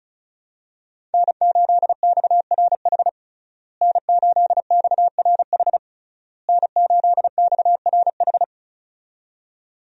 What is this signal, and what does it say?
Morse code, heard as a keyed tone, about 36 wpm, 700 Hz. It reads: N8XRH N8XRH N8XRH
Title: USA Call Signs - 5 Characters - Repeated 3x - 35wpm